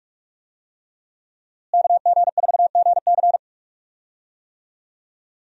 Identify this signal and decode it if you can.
Morse code, heard as a keyed tone, about 42 wpm, 700 Hz. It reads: KG4GC
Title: USA Call Signs - 5 Characters - Repeated 3x - 45wpm